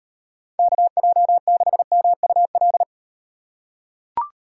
Morse code, heard as a keyed tone, about 38 wpm, 700 Hz. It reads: KJ6MUL